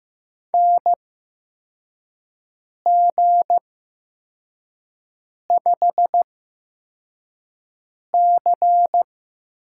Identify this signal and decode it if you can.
Morse code, heard as a keyed tone, about 15 wpm, spaced as for 4 wpm, 700 Hz. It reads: NG5C
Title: USA Call Signs - 8x Character Spacing - 15wpm